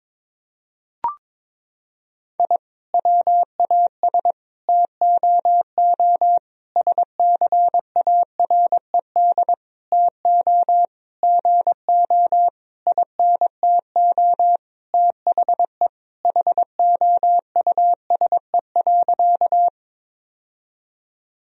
Morse code, heard as a keyed tone, about 22 wpm, 700 Hz. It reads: I WAS TOO SCARED TO GO INTO THE HOUSE.
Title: Sentences from Top 1000 Words - Review Part 4 - 22wpm